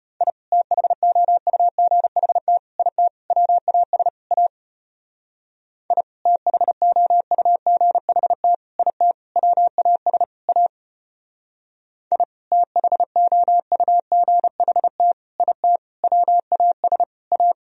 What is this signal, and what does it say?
Morse code, heard as a keyed tone, about 32 wpm, 700 Hz. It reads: I THOUGHT IT WAS A I THOUGHT IT WAS A I THOUGHT IT WAS A